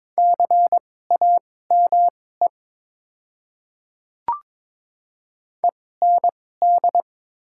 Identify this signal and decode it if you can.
Morse code, heard as a keyed tone, about 22 wpm, spaced as for 16 wpm, 700 Hz. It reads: CAME END
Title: 2x Character Spacing - Top 200 Words - 22wpm